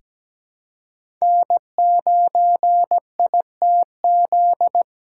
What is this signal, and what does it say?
Morse code, heard as a keyed tone, about 17 wpm, 700 Hz. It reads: N9ITZ
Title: USA Callsign DE Callsign - 5 Characters - 17wpm